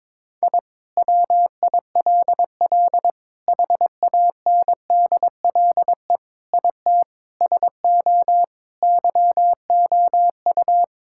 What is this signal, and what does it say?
Morse code, heard as a keyed tone, about 22 wpm, 700 Hz. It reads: I WILL HANDLE IT SO YOU
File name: Sentences with 7-Word Phrases - 22wpm